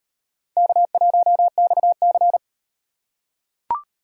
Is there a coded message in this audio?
Yes: Morse code from 0.6 to 2.4 s, heard as a keyed tone, about 38 wpm, 700 Hz. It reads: K1XC